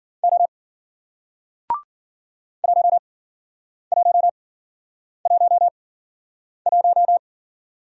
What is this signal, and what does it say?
Morse code, heard as a keyed tone, about 49 wpm, spaced as for 9 wpm, 700 Hz. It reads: O 1111
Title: Speed-Racing Single Letter-Number 40wpm